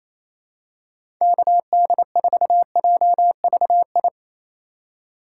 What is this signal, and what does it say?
Morse code, heard as a keyed tone, about 28 wpm, 700 Hz. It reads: KD4JVI
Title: USA Call Signs - 6 Characters - 8x Character Spacing - 28wpm